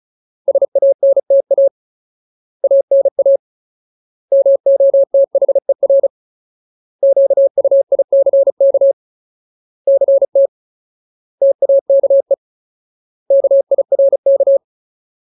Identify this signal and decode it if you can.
Morse code, heard as a keyed tone, about 34 wpm, 550 Hz. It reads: SANTA ANA MOTHER QUICK CT TAKE KIRK
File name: Ultimate Word List / 35WPM Ultimate